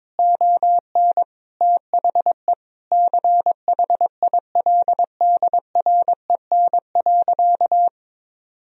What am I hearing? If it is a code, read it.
Morse code, heard as a keyed tone, about 22 wpm, 700 Hz. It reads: ON THE CHILDREN.
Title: Sentences from Top 1000 Words - Review Part 3 - 22wpm